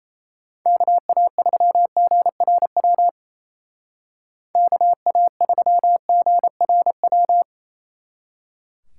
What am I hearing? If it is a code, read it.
Morse code, heard as a keyed tone, about 37 wpm, 700 Hz. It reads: KA3GRW KA3GRW